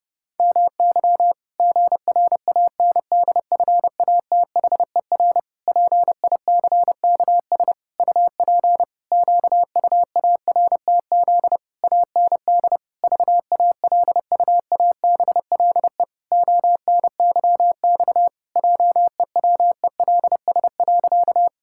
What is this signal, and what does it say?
Morse code, heard as a keyed tone, about 30 wpm, 700 Hz. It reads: MY GRANDFATHER PICKS UP QUARTZ AND VALUABLE ONYX JEWELS.
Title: Pangram Challenge v2 - 30wpm